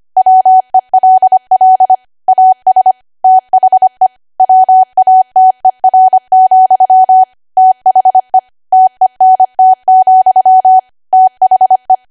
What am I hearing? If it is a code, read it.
Morse code, heard as a keyed tone, about 25 wpm, 750 Hz. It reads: WELL AS THE WATER, THE TENT, THE
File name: Archive / 25 WPM / 25WPM